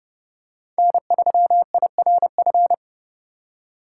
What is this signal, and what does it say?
Morse code, heard as a keyed tone, about 32 wpm, 700 Hz. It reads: N3IRF
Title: USA Call Signs - 6x Character Spacing - 30wpm